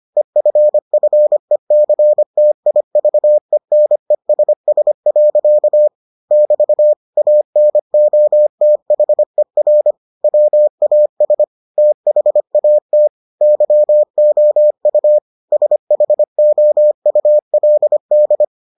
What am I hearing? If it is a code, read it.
Morse code, heard as a keyed tone, about 25 wpm, 600 Hz. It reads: EFFECTIVENESS. <BT> ANOTHER WAS THAT YOU SHOULD